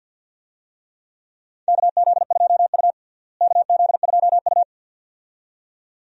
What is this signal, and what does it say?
Morse code, heard as a keyed tone, about 47 wpm, 700 Hz. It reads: KZ2U KZ2U
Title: USA Call Signs - 4 Characters - Repeated 2x - 50wpm